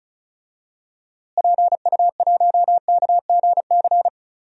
Morse code, heard as a keyed tone, about 34 wpm, 700 Hz. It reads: PU1KGC
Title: DXCC-100 Call Signs - 6 Characters - Repeated 3x - 35wpm